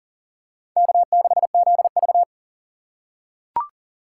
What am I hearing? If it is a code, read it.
Morse code, heard as a keyed tone, about 40 wpm, 700 Hz. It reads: K6ZV
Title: USA Call Signs - 4 Characters - 3x Character Spacing - 40wpm